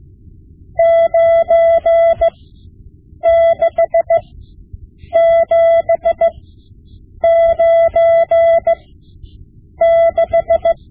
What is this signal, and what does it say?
Morse code, heard as a keyed tone, about 16 wpm, spaced as for 7 wpm, 670 Hz. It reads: 96796
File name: Morse Lessons / Morse Lessons / Tape-5B